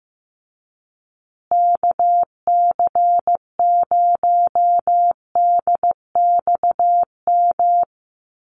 Morse code, heard as a keyed tone, about 15 wpm, 700 Hz. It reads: KC0DXM
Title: USA Call Signs Repeated 2x 15wpm